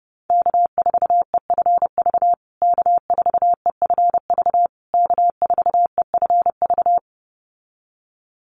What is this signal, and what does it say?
Morse code, heard as a keyed tone, about 30 wpm, 700 Hz. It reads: K4EFV K4EFV K4EFV
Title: USA Call Signs Repeated 3x 30wpm